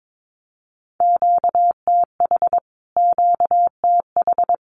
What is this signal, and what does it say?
Morse code, heard as a keyed tone, about 22 wpm, 700 Hz. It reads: QTH QTH
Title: Q-Codes Repeated 3x 22wpm